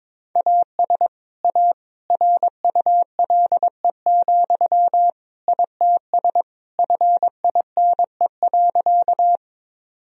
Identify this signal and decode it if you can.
Morse code, heard as a keyed tone, about 21 wpm, 700 Hz. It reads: AS A RULE, ITS FINE.
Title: Sentences from Top 1000 Words - Review Part 2 - 22wpm